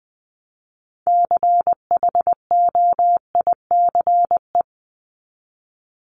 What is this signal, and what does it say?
Morse code, heard as a keyed tone, about 20 wpm, 700 Hz. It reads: CHOICE
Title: Top 751-1000 Words 20wpm